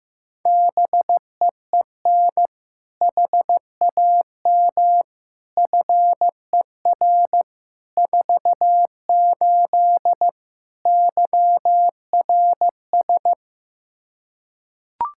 Morse code, heard as a keyed tone, about 15 wpm, 700 Hz. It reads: BEEN HAM FER 48 YRS
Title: Practice QSO - Exchange 3 - More Info - 15wpm